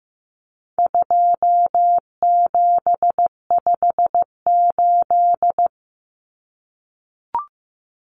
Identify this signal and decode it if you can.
Morse code, heard as a keyed tone, about 15 wpm, 700 Hz. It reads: 2758
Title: Numbers - 4-Digit 15wpm